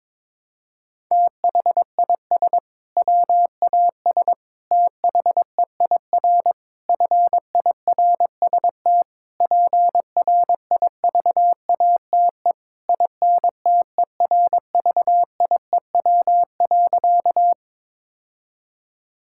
Morse code, heard as a keyed tone, about 22 wpm, 700 Hz. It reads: THIS WAS THEIR FIRST PRIVATE INTERVIEW.
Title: Sentences from Top 1600 Words 22wpm